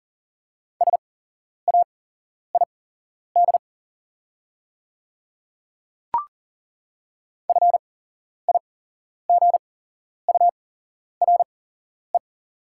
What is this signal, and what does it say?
Morse code, heard as a keyed tone, about 40 wpm, spaced as for 11 wpm, 690 Hz. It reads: SAID FIGURE
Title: 8x Character Spacing - Top 500 Words - 40wpm